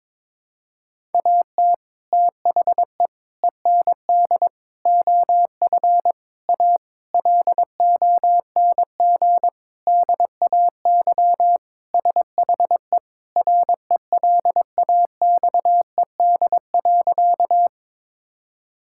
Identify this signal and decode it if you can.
Morse code, heard as a keyed tone, about 22 wpm, 700 Hz. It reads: AT THE END OF A LONG DAY SHE RELAXED.
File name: practice / 22wpm / Sentences with 4-Word Phrases - 22wpm